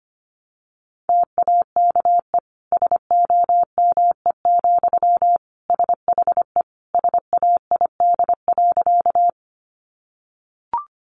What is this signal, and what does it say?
Morse code, heard as a keyed tone, about 25 wpm, 700 Hz. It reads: TAKE SOME, SHE SAID.
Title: Sentences from Top 200 Words 25wpm